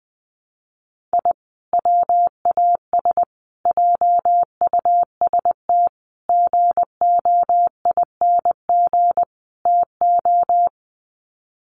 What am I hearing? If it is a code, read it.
Morse code, heard as a keyed tone, about 20 wpm, 700 Hz. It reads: I WAS JUST GOING TO
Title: Sets of 5 Words Encore 20wpm